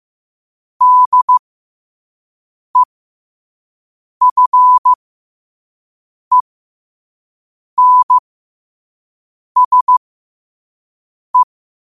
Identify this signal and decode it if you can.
Morse code, heard as a keyed tone, about 15 wpm, spaced as for 5 wpm, 1000 Hz. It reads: DEFENSE
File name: qotdfiles / 5wpm